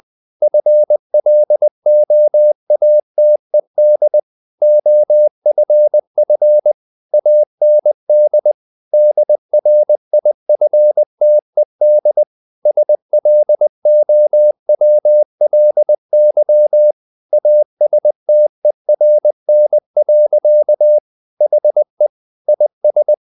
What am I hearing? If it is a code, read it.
Morse code, heard as a keyed tone, about 20 wpm, 600 Hz. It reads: FLOATED OFF AND DRIFTED SLOWLY ASTERN. HE IS
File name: the old man and the sea part 2 20wpm